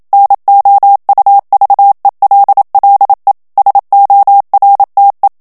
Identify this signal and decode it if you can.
Morse code, heard as a keyed tone, about 27 wpm, 800 Hz. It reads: NOUVELLE SORTE
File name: FRANCAIS TEXTE CLAIR 2 VITESSE 26 MOTS MINUTES